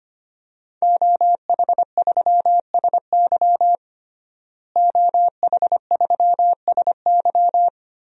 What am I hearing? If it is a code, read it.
Morse code, heard as a keyed tone, about 25 wpm, 700 Hz. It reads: OH3SY OH3SY
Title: DXCC-100 Call Signs - 5 Characters - Repeated 2x - 25wpm